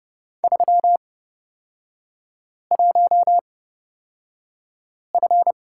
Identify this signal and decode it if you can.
Morse code, heard as a keyed tone, about 30 wpm, spaced as for 5 wpm, 700 Hz. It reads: 31F
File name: Lesson 36e - BT - Three Characters 30wpm